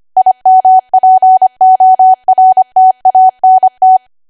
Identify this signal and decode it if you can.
Morse code, heard as a keyed tone, about 25 wpm, 750 Hz. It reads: IMPORTANT
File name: Archive / 25 WPM / 25WPM